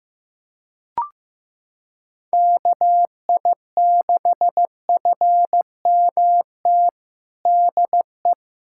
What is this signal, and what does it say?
Morse code, heard as a keyed tone, about 16 wpm, 700 Hz. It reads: KI6FMT DE
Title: USA Callsign DE Callsign - 6 Characters - 15wpm